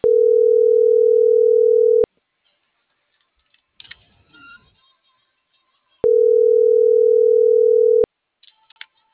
Ringback tone (North America), 440 + 480 Hz.